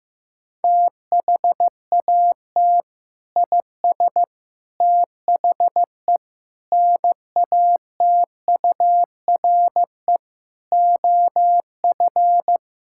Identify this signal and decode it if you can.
Morse code, heard as a keyed tone, about 15 wpm, 700 Hz. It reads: THAT IS THE NATURE OF
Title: Sentences from Top 1000 Words - Review Part 5 - 15wpm